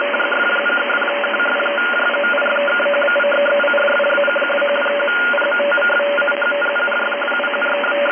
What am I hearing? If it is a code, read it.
Two-tone teleprinter signal, about 850 Hz shift.
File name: rec / 29-Dec-2012-2338 2.005MHz